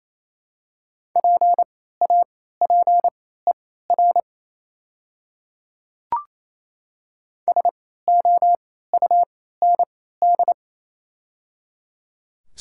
Morse code, heard as a keyed tone, about 28 wpm, spaced as for 16 wpm, 700 Hz. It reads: PAPER SOUND
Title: 3x Character Spacing - Top 500 Words - 28wpm